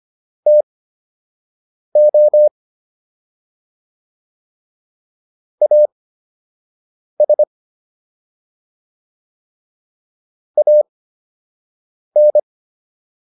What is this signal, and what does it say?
Morse code, heard as a keyed tone, about 25 wpm, spaced as for 6 wpm, 600 Hz. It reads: TO AS AN